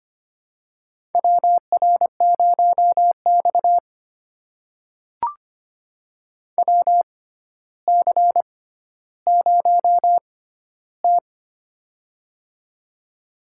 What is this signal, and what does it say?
Morse code, heard as a keyed tone, about 25 wpm, spaced as for 13 wpm, 700 Hz. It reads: WR0X WC0T
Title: USA Call Signs - 4 Characters - 6x Character Spacing - 25wpm